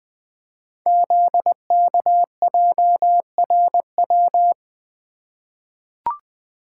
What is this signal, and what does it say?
Morse code, heard as a keyed tone, about 20 wpm, 700 Hz. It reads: ZKJRW
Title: Code Groups - 5 Letter:Numbers - 20wpm